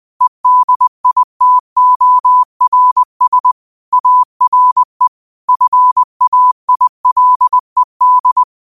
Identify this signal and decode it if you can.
Morse code, heard as a keyed tone, about 20 wpm, 1000 Hz. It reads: EDITORS ARE FAILED
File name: qotdfiles / 20wpm